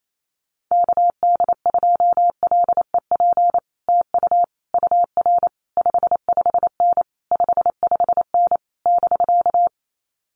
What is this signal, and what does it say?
Morse code, heard as a keyed tone, about 28 wpm, 700 Hz. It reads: KD2LEP TU UR 55N 55N <BK>
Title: Common QSO Phrases Encore 28wpm